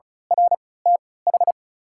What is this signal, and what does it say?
Morse code, heard as a keyed tone, about 36 wpm, spaced as for 20 wpm, 700 Hz. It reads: RTH
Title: 3x Character Spacing - Top 200 Words - 35wpm